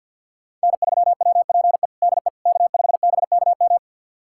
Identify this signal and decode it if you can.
Morse code, heard as a keyed tone, about 52 wpm, 700 Hz. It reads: N3WPE DE K5BKM
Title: USA Callsign DE Callsign - 5 Characters - 50wpm